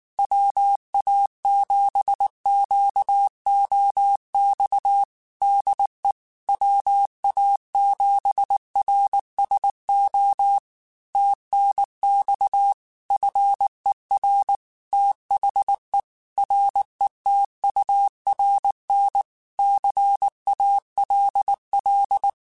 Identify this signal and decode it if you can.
Morse code, heard as a keyed tone, about 19 wpm, 800 Hz. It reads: WA7QOX DE WA7RSO TNX FER THE RETURN CALL
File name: QSOs / QSOs-mp3s / WA7QOX 2@18